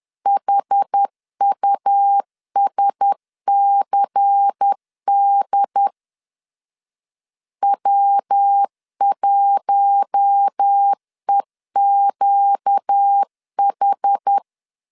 Morse code, heard as a keyed tone, about 10 wpm, 800 Hz. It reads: HUSCD W1EQH
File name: LEZIONI / 7 GF9Z / 7 GF9Z 10wpm 2611